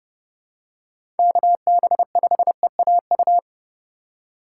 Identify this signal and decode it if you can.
Morse code, heard as a keyed tone, about 30 wpm, 700 Hz. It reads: KB5EAU
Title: USA Call Signs - 6 Characters - 4x Character Spacing - 30wpm